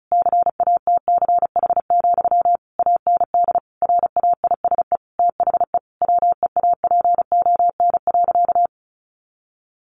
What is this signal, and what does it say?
Morse code, heard as a keyed tone, about 34 wpm, 700 Hz. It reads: CATCH, AND RAISE THE WEAPON.
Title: A Sound Of Thunder 35wpm